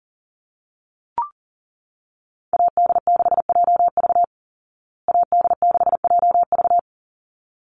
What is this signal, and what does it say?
Morse code, heard as a keyed tone, about 40 wpm, 700 Hz. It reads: AD6JV AD6JV
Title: Call Signs Repeated 3x - 40wpm